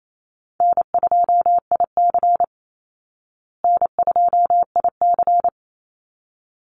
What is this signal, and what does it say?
Morse code, heard as a keyed tone, about 28 wpm, 700 Hz. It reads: N2IC N2IC
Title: Easy Call Signs Repeated 2x - 28wpm